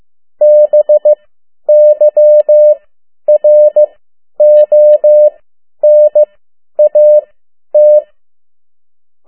Morse code, heard as a keyed tone, about 15 wpm, spaced as for 10 wpm, 600 Hz. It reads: BYRON AT
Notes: Morse code 10 WPM